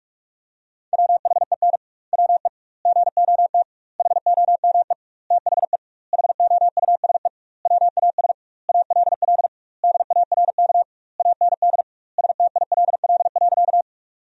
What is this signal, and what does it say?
Morse code, heard as a keyed tone, about 41 wpm, 700 Hz. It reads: WHEN WE GOT HOME THE HOUSE WAS ALL DARK AND STILL.
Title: 2x Word-Spacing - Sentences from Top 500 Words - 45wpm